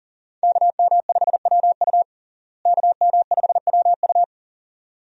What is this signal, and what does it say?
Morse code, heard as a keyed tone, about 37 wpm, 700 Hz. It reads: KM5WU KM5WU
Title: USA Call Signs - 5 Characters - Repeated 2x - 40wpm